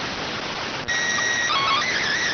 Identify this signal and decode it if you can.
SSTV (slow-scan television), PD 290.